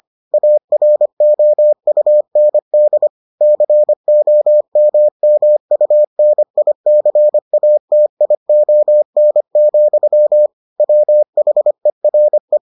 Morse code, heard as a keyed tone, about 25 wpm, 600 Hz. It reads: AROUND COMMUNICATION, WHERE